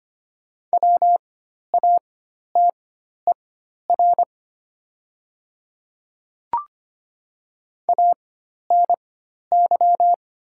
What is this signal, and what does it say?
Morse code, heard as a keyed tone, about 25 wpm, spaced as for 12 wpm, 700 Hz. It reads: WATER ANY
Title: 4x Character Spacing - Top 200 Words - 25wpm